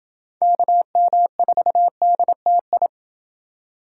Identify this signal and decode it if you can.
Morse code, heard as a keyed tone, about 29 wpm, 700 Hz. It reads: KM4DTI